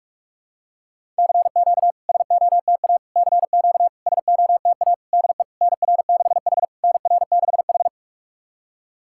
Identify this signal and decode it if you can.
Morse code, heard as a keyed tone, about 46 wpm, 700 Hz. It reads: CQ SOTA CQ SOTA DE NR6H NR6H
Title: SOTA QSO Practice 45wpm